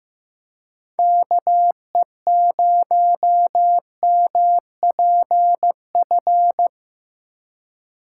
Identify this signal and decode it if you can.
Morse code, heard as a keyed tone, about 15 wpm, 700 Hz. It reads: KE0MPF
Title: USA Call Signs - 6 Characters - 2x Character Spacing - 15wpm